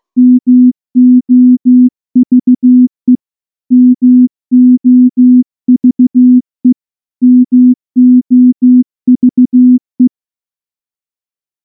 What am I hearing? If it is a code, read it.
Morse code, heard as a keyed tone, about 14 wpm, 250 Hz. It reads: MOVE MOVE MOVE
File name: morse / morse-training-15WPM